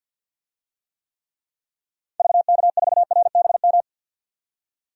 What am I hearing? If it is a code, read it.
Morse code, heard as a keyed tone, about 52 wpm, 700 Hz. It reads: VK4RBM
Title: DXCC-100 Call Signs - 6 Characters - 50wpm